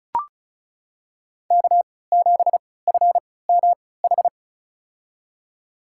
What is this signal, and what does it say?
Morse code, heard as a keyed tone, about 34 wpm, spaced as for 20 wpm, 700 Hz. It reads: K7FMH
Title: USA Call Signs - 5 Characters - 3x Character Spacing - 35wpm